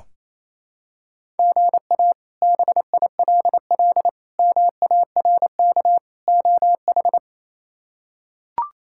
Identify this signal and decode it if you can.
Morse code, heard as a keyed tone, about 28 wpm, 700 Hz. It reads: GA BILL MARK OH
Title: K1USN SST QSO Practice - 28wpm